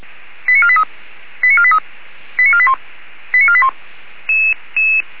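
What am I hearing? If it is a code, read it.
Five-tone selective call (ZVEI), heard as quick runs of five tones, tones of about 70 ms. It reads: 89483 89483 89481 89481
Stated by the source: ZVEI five-tone call 89257 received by SDR.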